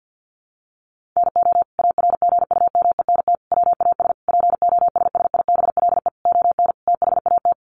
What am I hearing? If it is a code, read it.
Morse code, heard as a keyed tone, about 50 wpm, 700 Hz. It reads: NO ARGUMENT WAS POSSIBLE ON THAT
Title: Sentences from Top 1400 Words 50wpm